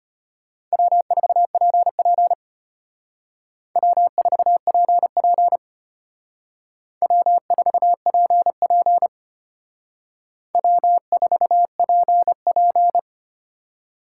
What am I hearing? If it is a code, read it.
Morse code, heard as a keyed tone, about 32 wpm, 700 Hz. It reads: W4PP W4PP W4PP W4PP